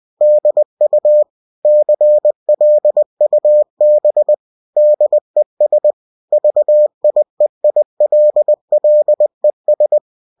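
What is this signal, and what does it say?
Morse code, heard as a keyed tone, about 20 wpm, 600 Hz. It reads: DU CLUB DES VIEILLES
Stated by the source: Bulletin sent in CW.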